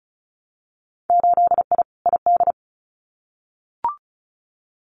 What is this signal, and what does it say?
Morse code, heard as a keyed tone, about 36 wpm, 700 Hz. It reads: 8I ID
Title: Winter Field Day Exchanges Encore 35wpm